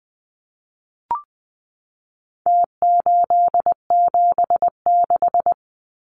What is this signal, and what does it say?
Morse code, heard as a keyed tone, about 20 wpm, 700 Hz. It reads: T876
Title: Numbers - Contest Style - 4-Digit 20wpm